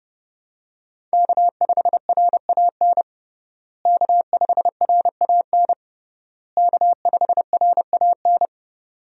Morse code, heard as a keyed tone, about 30 wpm, 700 Hz. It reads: K5RAN K5RAN K5RAN
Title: USA Call Signs - 5 Characters - Repeated 3x - 30wpm